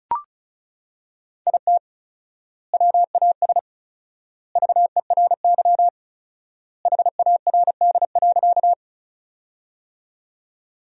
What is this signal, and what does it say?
Morse code, heard as a keyed tone, about 34 wpm, 700 Hz. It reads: IT WAS VERY HARD.
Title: 4x Word-Spacing - Sentences from Top 300 Words - 35wpm